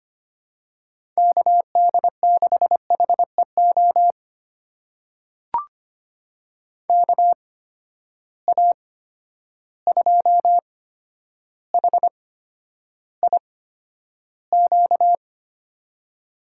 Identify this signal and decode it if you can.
Morse code, heard as a keyed tone, about 25 wpm, spaced as for 11 wpm, 700 Hz. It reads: KD6HEO KA2HIQ
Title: USA Call Signs - 6 Characters - 8x Character Spacing - 25wpm